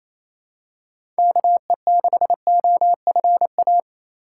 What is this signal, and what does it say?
Morse code, heard as a keyed tone, about 28 wpm, 700 Hz. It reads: KE6OFA
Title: USA Call Signs - 6 Characters - 4x Character Spacing - 28wpm